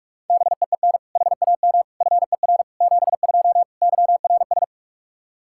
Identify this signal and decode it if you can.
Morse code, heard as a keyed tone, about 46 wpm, 700 Hz. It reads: BEEN HAM FER 72 YRS
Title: Practice QSO - Exchange 3 - More Info - 45wpm